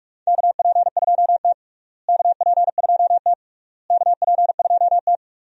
Morse code, heard as a keyed tone, about 46 wpm, 700 Hz. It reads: KP2T KP2T KP2T
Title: USA Call Signs - 4 Characters - Repeated 3x - 45wpm